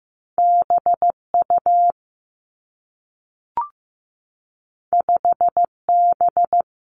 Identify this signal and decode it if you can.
Morse code, heard as a keyed tone, about 15 wpm, 700 Hz. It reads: B U 5 B
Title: Two Challenging Letter-Number Combinations 15wpm